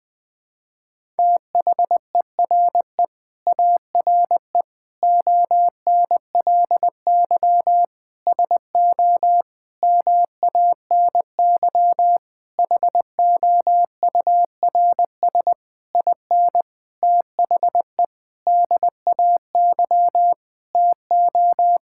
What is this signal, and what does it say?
Morse code, heard as a keyed tone, about 20 wpm, 700 Hz. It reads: THERE ARE ONLY SO MANY HOURS IN THE DAY TO
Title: Sentences with 9-Word Phrases - Spoken Beginning - 20wpm